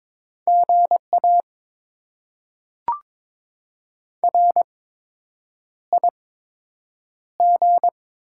Morse code, heard as a keyed tone, about 22 wpm, spaced as for 8 wpm, 700 Hz. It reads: GA RIG
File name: Top 20 QSO Elements - 8x Character Spacing - 22wpm